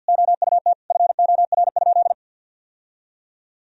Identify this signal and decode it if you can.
Morse code, heard as a keyed tone, about 48 wpm, 700 Hz. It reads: OUT FOR?
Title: Sentences from Top 1000 Words - Review Part 3 - 50wpm